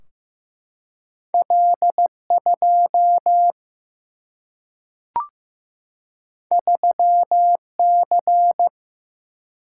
Morse code, heard as a keyed tone, about 15 wpm, 700 Hz. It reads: L 2 3 C
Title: Single-Letter Single-Number 15wpm